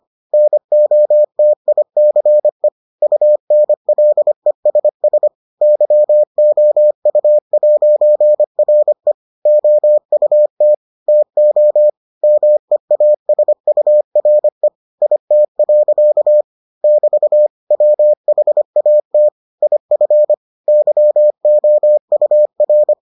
Morse code, heard as a keyed tone, about 25 wpm, 600 Hz. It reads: NOTICE UNLESS YOU'RE OUT TO MEASURE IT. <BT> WHAT IF YOUR